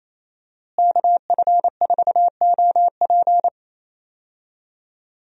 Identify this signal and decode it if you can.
Morse code, heard as a keyed tone, about 28 wpm, 700 Hz. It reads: KF4OP
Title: USA Call Signs - 5 Characters - Repeated 3x - 28wpm